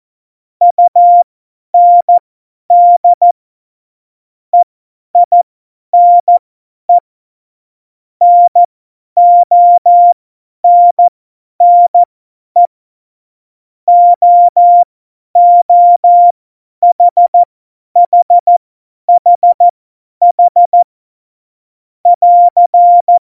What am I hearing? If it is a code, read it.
Morse code, heard as a keyed tone, about 14 wpm, spaced as for 10 wpm, 700 Hz. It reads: UND EINE NONNE OOHHHH <AR>
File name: Morsekurs / Witze / Witz 12. Januar 2021, 50 F70